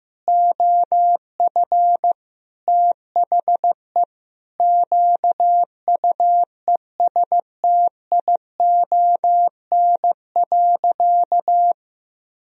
Morse code, heard as a keyed tone, about 15 wpm, 700 Hz. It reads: OF THE QUESTION.
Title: Sentences from Top 400 Words and Abbreviations 15wpm